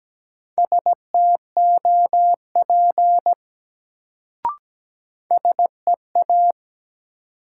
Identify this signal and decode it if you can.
Morse code, heard as a keyed tone, about 17 wpm, 700 Hz. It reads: STOP SEA
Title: Sentences from Top 300 Words - New Word Drill - 17wpm